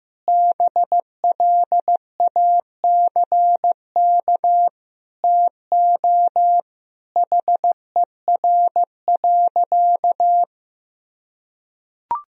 Morse code, heard as a keyed tone, about 15 wpm, 700 Hz. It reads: BLACK TO HER.
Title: Sentences from Top 1000 Words - Review Part 2 - 15wpm